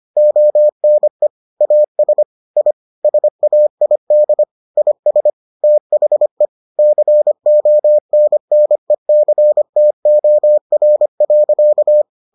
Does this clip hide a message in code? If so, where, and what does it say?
Yes: Morse code from 0.2 to 12.0 s, heard as a keyed tone, about 25 wpm, 600 Hz. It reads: ONE AS I SAID IS THE CONNECTOR.